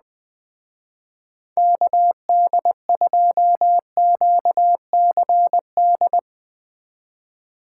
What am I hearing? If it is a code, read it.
Morse code, heard as a keyed tone, about 20 wpm, 700 Hz. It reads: KD2QCD
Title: Speed Racing - USA 2x3 Call Signs 17wpm